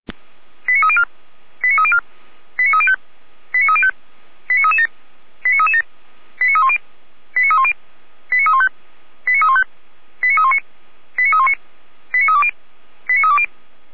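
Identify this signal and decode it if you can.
Five-tone selective call (ZVEI), heard as quick runs of five tones, tones of about 70 ms. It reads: 89304 89304 89305 89305 89337 89337 89310 89310 89315 89315 89319 89319 89320 89320
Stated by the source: ZVEI five-tone call 89262 received by SDR, from feuerwehr-orlamuende (fire brigade Orlamünde).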